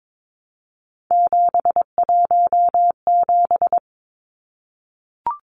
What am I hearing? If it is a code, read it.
Morse code, heard as a keyed tone, about 22 wpm, 700 Hz. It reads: 717
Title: Numbers - 3-Digit 22wpm